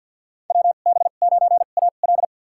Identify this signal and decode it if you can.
Morse code, heard as a keyed tone, about 52 wpm, spaced as for 37 wpm, 690 Hz. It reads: WB9AL
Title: USA Call Signs - 5 Characters - 2x Character Spacing - 50wpm